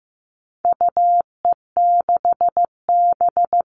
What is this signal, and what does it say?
Morse code, heard as a keyed tone, about 15 wpm, 700 Hz. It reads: UE6B
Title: All Cut Numbers - 4-Digit 15wpm